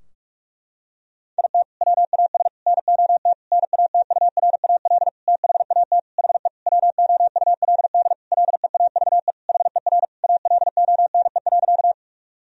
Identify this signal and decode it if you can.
Morse code, heard as a keyed tone, about 46 wpm, 700 Hz. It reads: IT WAS NOT NATURAL THAT HE WOULD LEAVE HER ALONE.
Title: Sentences from Top 1000 Words - Review Part 4 - 45wpm